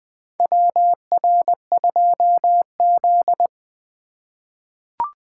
Morse code, heard as a keyed tone, about 20 wpm, 700 Hz. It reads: WR2Z
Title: USA Call Signs - 3x Character Spacing - 20wpm